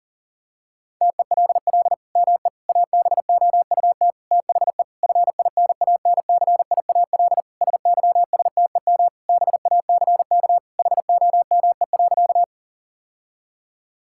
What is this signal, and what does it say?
Morse code, heard as a keyed tone, about 40 wpm, 700 Hz. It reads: TELL ME ABOUT THE FINANCIAL SYSTEM BACK HOME.
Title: Sentences from Top 1000 Words - Review Part 5 - 40wpm